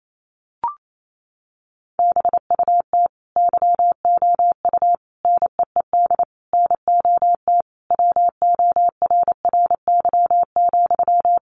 Morse code, heard as a keyed tone, about 28 wpm, 700 Hz. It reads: BUT YOU NEED NOT WORRY,
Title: Sentences from Top 1300 Words 28wpm